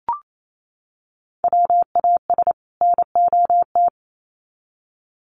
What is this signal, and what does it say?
Morse code, heard as a keyed tone, about 28 wpm, 700 Hz. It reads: WAS NOT
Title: Sets of 2 Words 28wpm